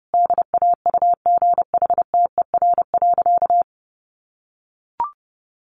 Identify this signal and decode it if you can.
Morse code, heard as a keyed tone, about 30 wpm, 700 Hz. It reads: DAUGHTER.
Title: Sentences from Top 1500 Words 30wpm